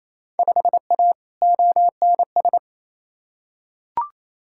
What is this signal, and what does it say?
Morse code, heard as a keyed tone, about 28 wpm, 700 Hz. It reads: 5A ONS
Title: Field Day Exchanges Encore - 28wpm